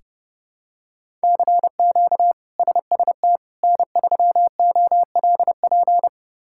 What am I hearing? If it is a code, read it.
Morse code, heard as a keyed tone, about 31 wpm, 700 Hz. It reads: CQ SST N3OLP
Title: K1USN SST QSO Practice - 30wpm